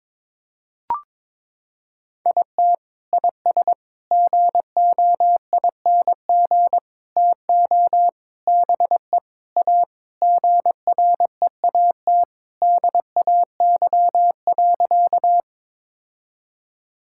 Morse code, heard as a keyed tone, about 22 wpm, 700 Hz. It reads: IT IS GOING TO BE A GREAT DAY.
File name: practice / 22wpm / Sentences with 3-Word Phrases - 22wpm